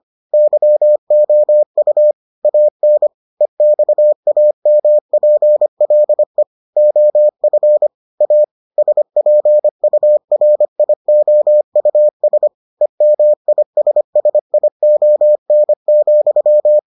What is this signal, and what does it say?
Morse code, heard as a keyed tone, about 25 wpm, 600 Hz. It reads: YOU AN EXAMPLE OF A SPURIOUS EMISSION,